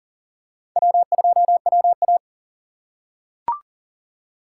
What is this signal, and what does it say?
Morse code, heard as a keyed tone, about 37 wpm, 700 Hz. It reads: W2WA
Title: USA Call Signs - 4 Characters - 6x Character Spacing - 40wpm